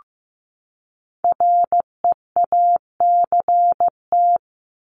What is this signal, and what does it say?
Morse code, heard as a keyed tone, about 15 wpm, 700 Hz. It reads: REACT
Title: Top 1251-1500 Words 15wpm